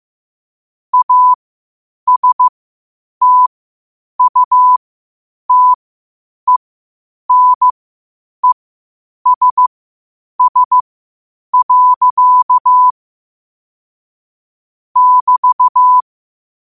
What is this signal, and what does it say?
Morse code, heard as a keyed tone, about 15 wpm, spaced as for 8 wpm, 1000 Hz. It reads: ASTUTENESS. <BT>
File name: qotdfiles / 8wpm